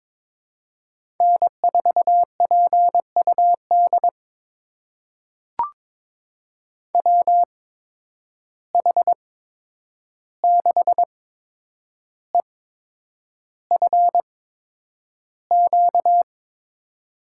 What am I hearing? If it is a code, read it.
Morse code, heard as a keyed tone, about 22 wpm, spaced as for 9 wpm, 700 Hz. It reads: N4PUD WH6EFQ
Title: USA Call Signs - 8x Character Spacing - 22wpm